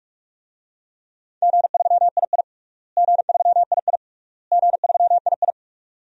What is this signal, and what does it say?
Morse code, heard as a keyed tone, about 46 wpm, 700 Hz. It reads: G3II G3II G3II
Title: DXCC-100 Call Signs - 4 Characters - Repeated 3x - 45wpm